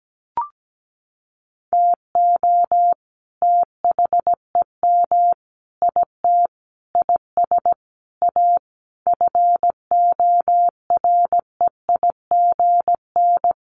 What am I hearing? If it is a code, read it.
Morse code, heard as a keyed tone, about 17 wpm, 700 Hz. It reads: TO THEM IT IS A FOREIGN
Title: Sentences from Top 1400 Words 17wpm